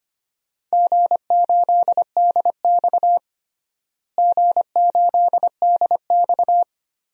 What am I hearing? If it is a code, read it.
Morse code, heard as a keyed tone, about 25 wpm, 700 Hz. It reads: G8DX G8DX
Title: DXCC-100 Call Signs - 4 Characters - Repeated 2x - 25wpm